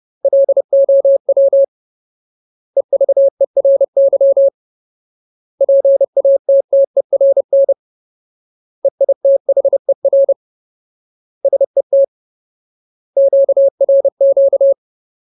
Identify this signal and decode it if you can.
Morse code, heard as a keyed tone, about 30 wpm, 550 Hz. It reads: LOW EVERY PATTERN EITHER SET QRQ
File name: Ultimate Word List / 30WPM Ultimate